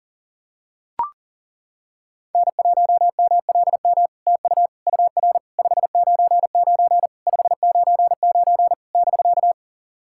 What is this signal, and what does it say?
Morse code, heard as a keyed tone, about 40 wpm, 700 Hz. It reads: N1MLM TU UR 599 599 <BK>
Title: SOTA QSO Practice 40wpm